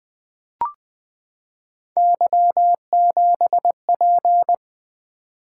Morse code, heard as a keyed tone, about 20 wpm, 700 Hz. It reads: Y7P
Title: Three Letter-Number Combinations - 20wpm